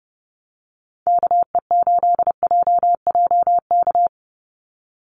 Morse code, heard as a keyed tone, about 29 wpm, 700 Hz. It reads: KE8JJK
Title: USA Call Signs 30wpm